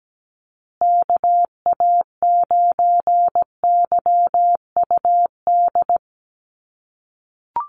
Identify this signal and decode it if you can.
Morse code, heard as a keyed tone, about 17 wpm, 700 Hz. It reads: KA9YUD
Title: USA Call Signs Repeated 3x 17wpm